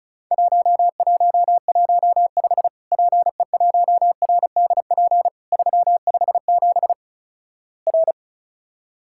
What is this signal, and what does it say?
Morse code, heard as a keyed tone, about 36 wpm, 700 Hz. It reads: 1115 PE1RDP 357 R
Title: Worked All Europe DX Contest - QTC Exchange - 35wpm